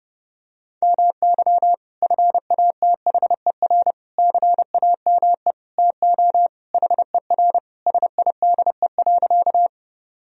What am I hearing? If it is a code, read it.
Morse code, heard as a keyed tone, about 31 wpm, 700 Hz. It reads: MY FATHER CAME TO HER SIDE.
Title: Sentences from Top 1000 Words - Review Part 1 - 30wpm